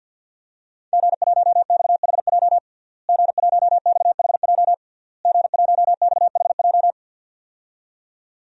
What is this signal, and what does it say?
Morse code, heard as a keyed tone, about 52 wpm, 700 Hz. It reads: G1XHJ G1XHJ G1XHJ
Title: DXCC-100 Call Signs - 5 Characters - Repeated 3x - 50wpm